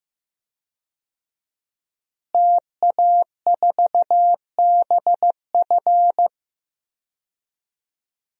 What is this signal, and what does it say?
Morse code, heard as a keyed tone, about 15 wpm, 700 Hz. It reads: TA4BF
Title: DXCC-100 Call Signs - 5 Characters - Repeated 2x - 15wpm